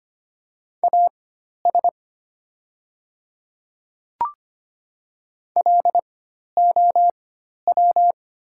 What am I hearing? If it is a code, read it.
Morse code, heard as a keyed tone, about 25 wpm, spaced as for 12 wpm, 700 Hz. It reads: AS LOW
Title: 4x Character Spacing - Top 200 Words - 25wpm